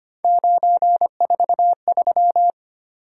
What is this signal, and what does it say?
Morse code, heard as a keyed tone, about 25 wpm, 700 Hz. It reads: 943
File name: Three Letter-Number Combinations - 25wpm